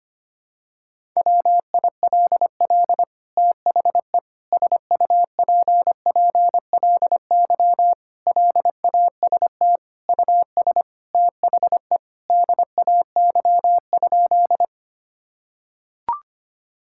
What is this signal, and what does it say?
Morse code, heard as a keyed tone, about 25 wpm, 700 Hz. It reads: WILL THE SUPPLY LAST US THE DAY?
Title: Sentences from Top 1000 Words - Review Part 5 - 25wpm